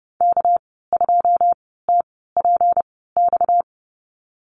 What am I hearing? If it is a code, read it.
Morse code, heard as a keyed tone, about 30 wpm, spaced as for 17 wpm, 700 Hz. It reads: K2TPX
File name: Lesson 37g - Letter x - Call Signs 30wpm